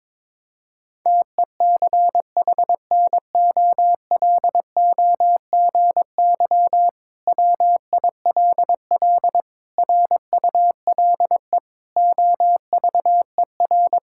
Morse code, heard as a keyed tone, about 21 wpm, 700 Hz. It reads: TECHNOLOGY WILL RULE OVER
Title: Sentences from Top 1000 Words - Review Part 5 - 22wpm